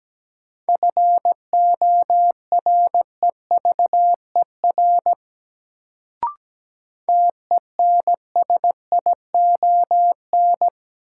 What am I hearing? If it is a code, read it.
Morse code, heard as a keyed tone, about 17 wpm, 700 Hz. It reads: FOREVER TENSION
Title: Sentences from Top 1600 Words - New Word Drill - 17wpm